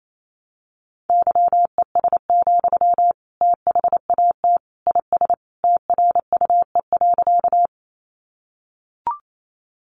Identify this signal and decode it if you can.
Morse code, heard as a keyed tone, about 28 wpm, 700 Hz. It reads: YES, THAT IS TRUE.
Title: Sentences from Top 500 Words 28wpm